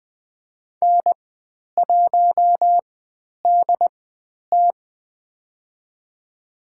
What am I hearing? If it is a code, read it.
Morse code, heard as a keyed tone, about 20 wpm, spaced as for 10 wpm, 700 Hz. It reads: N1DT
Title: 102 Lesson 17g - Letter u - Call Signs 20wpm